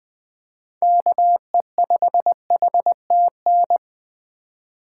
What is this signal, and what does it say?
Morse code, heard as a keyed tone, about 20 wpm, 700 Hz. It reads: KE5HTN
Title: USA Call Signs - 4x Character Spacing - 20wpm